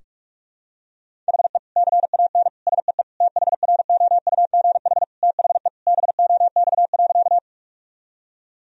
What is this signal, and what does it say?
Morse code, heard as a keyed tone, about 46 wpm, 700 Hz. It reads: HE CAN SEE THROUGH THE BOX.
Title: Sentences from Top 1000 Words - Review Part 2 - 45wpm